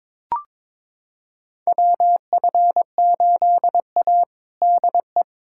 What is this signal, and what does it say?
Morse code, heard as a keyed tone, about 22 wpm, 700 Hz. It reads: WF8A DE
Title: USA Callsign DE Callsign - 4 Characters - 22wpm